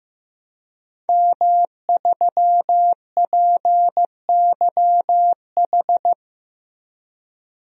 Morse code, heard as a keyed tone, about 15 wpm, 700 Hz. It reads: M3PYH
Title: DXCC-100 Call Signs - 5 Characters - Repeated 3x - 15wpm